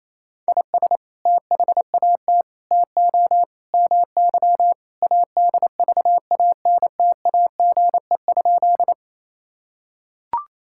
Morse code, heard as a keyed tone, about 28 wpm, 700 Hz. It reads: IS THAT TO MY ADVANTAGE?
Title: Sentences from Top 1000 Words - Review Part 6 - 28wpm